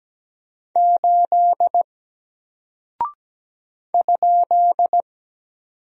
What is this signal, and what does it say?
Morse code, heard as a keyed tone, about 17 wpm, 700 Hz. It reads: 8 ?
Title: Single Character (Letter:Number:Punctuation:Prosign) Alt 17wpm